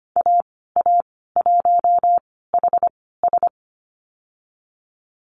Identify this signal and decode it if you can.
Morse code, heard as a keyed tone, about 25 wpm, spaced as for 16 wpm, 690 Hz. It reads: AA1HS
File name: Lesson 15g - Number 5 - Call Signs 25wpm